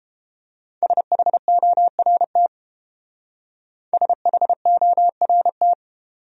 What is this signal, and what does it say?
Morse code, heard as a keyed tone, about 28 wpm, 700 Hz. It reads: SHORT SHORT